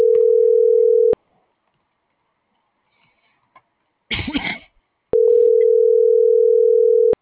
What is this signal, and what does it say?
Ringback tone (North America), 440 + 480 Hz.